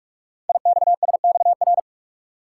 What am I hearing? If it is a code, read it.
Morse code, heard as a keyed tone, about 44 wpm, 700 Hz. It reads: IXSXR
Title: Code Groups - 5 Letter:Numbers - 45wpm